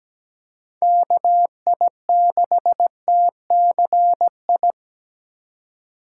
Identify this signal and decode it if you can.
Morse code, heard as a keyed tone, about 17 wpm, 700 Hz. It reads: KI6TCI
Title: USA Call Signs - 6 Characters - 4x Character Spacing - 17wpm